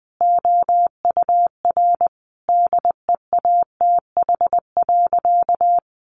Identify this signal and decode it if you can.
Morse code, heard as a keyed tone, about 20 wpm, 700 Hz. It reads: OUR DEATH.
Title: Sentences from Top 1200 Words 20wpm